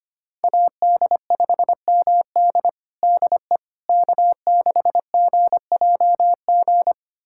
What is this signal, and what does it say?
Morse code, heard as a keyed tone, about 25 wpm, 700 Hz. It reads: AD5MD DE K6GJG
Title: USA Callsign DE Callsign - 5 Characters - 25wpm